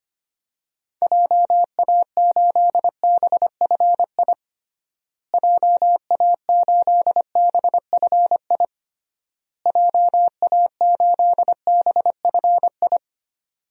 Morse code, heard as a keyed tone, about 25 wpm, 700 Hz. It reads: JA8BFI JA8BFI JA8BFI
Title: DXCC-100 Call Signs - 6 Characters - Repeated 3x - 25wpm